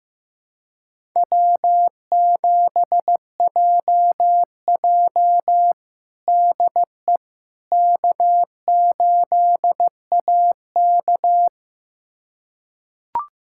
Morse code, heard as a keyed tone, about 15 wpm, 700 Hz. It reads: W7JJ DE K8AK
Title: USA Callsign DE Callsign - 4 Characters - 15wpm